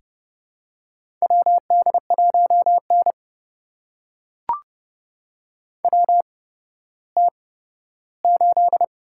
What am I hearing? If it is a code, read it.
Morse code, heard as a keyed tone, about 31 wpm, spaced as for 15 wpm, 700 Hz. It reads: WD1N WT8
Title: USA Call Signs - 4 Characters - 8x Character Spacing - 30wpm